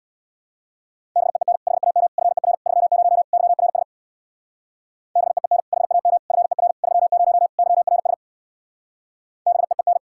Morse code, heard as a keyed tone, about 66 wpm, 700 Hz. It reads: BEEN HAM FER 30 YRS BEEN HAM FER 30 YRS BEEN